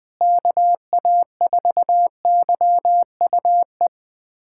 Morse code, heard as a keyed tone, about 21 wpm, 700 Hz. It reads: KA4YUE
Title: USA Callsign DE Callsign - 6 Characters - 20wpm